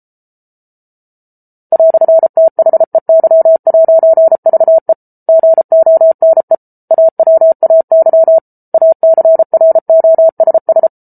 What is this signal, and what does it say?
Morse code, heard as a keyed tone, about 34 wpm, 650 Hz. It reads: "THEY'VE GONE AWAY ACROSS